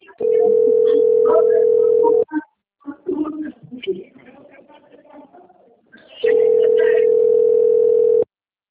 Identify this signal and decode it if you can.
Ringback tone (North America), 440 + 480 Hz, with speech.
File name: rg-1600-unknown- - - .23850